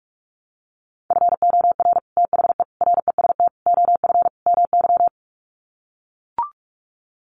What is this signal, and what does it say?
Morse code, heard as a keyed tone, about 44 wpm, 700 Hz. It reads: FOR THE REST OF MY
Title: Sets of 5 Words 45wpm